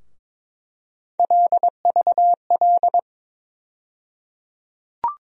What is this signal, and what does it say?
Morse code, heard as a keyed tone, about 22 wpm, 700 Hz. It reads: LVL
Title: USA Call Sign Postfixes - 3 Letters 22wpm